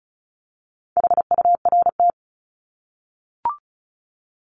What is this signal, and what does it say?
Morse code, heard as a keyed tone, about 34 wpm, 700 Hz. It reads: HURT
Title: Top 1751-2000 Words 35wpm